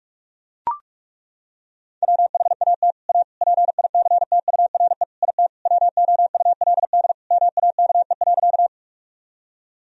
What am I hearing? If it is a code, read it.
Morse code, heard as a keyed tone, about 41 wpm, 700 Hz. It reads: WHAT A PICTURE IT WOULD MAKE.
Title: Sentences from Top 1000 Words - Review Part 1 - 45wpm